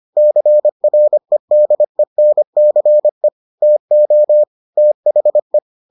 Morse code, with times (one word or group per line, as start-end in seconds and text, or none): 0.2-3.3 CREDENCE
3.6-4.4 TO
4.8-5.6 THE